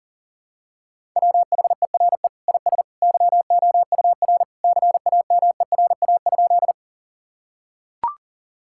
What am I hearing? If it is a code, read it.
Morse code, heard as a keyed tone, about 40 wpm, 700 Hz. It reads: WHERE IS YOUR CAMERA?
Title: Sentences from Top 1000 Words - Review Part 5 - 40wpm